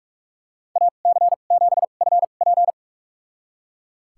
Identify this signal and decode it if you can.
Morse code, heard as a keyed tone, about 41 wpm, spaced as for 31 wpm, 700 Hz. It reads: AC7FP
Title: USA Call Signs - 2x Character Spacing - 45wpm